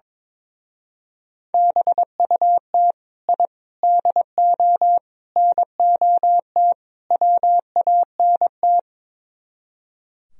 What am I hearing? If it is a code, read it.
Morse code, heard as a keyed tone, about 24 wpm, 700 Hz. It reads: BUT I DO NOT WANT
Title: Speed-Racing - Sets of 5 Words 22wpm